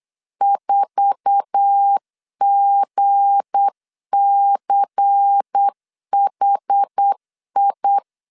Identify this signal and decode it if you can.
Morse code, heard as a keyed tone, about 8 wpm, 800 Hz. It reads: 4GCHI
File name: LEZIONI / 10 SIGNS / 10A SLASH 8wpm 3452